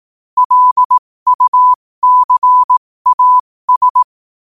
Morse code, heard as a keyed tone, about 18 wpm, spaced as for 15 wpm, 1000 Hz. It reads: LUCAS
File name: qotdfiles / 15wpm